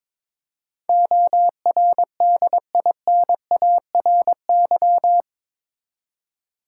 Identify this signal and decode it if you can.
Morse code, heard as a keyed tone, about 22 wpm, 700 Hz. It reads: ORDINARY
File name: practice / 22wpm / Eight Letter Words - 22wpm